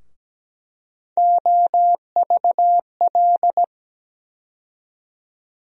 Morse code, heard as a keyed tone, about 17 wpm, 700 Hz. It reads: OVL
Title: USA Call Sign Postfixes - 3 Letters 17wpm